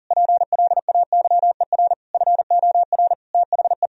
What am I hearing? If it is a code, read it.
Morse code, heard as a keyed tone, about 40 wpm, 700 Hz. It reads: PLAYER FOR THE
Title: Sentences from Top 1000 Words - Review Part 4 - 40wpm